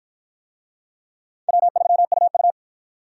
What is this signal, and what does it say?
Morse code, heard as a keyed tone, about 52 wpm, 700 Hz. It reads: W3UU